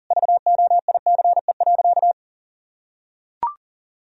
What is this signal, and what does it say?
Morse code, heard as a keyed tone, about 40 wpm, 700 Hz. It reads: VOICE.
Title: Sentences from Top 1000 Words - Review Part 1 - 40wpm